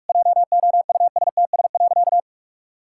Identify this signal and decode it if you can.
Morse code, heard as a keyed tone, about 41 wpm, 700 Hz. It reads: JOUSTS.
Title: Pangram Challenge v2 - 45wpm